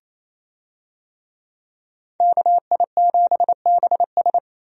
Morse code, heard as a keyed tone, about 28 wpm, 700 Hz. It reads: KI7BS
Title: USA Call Signs - 5 Characters - 28wpm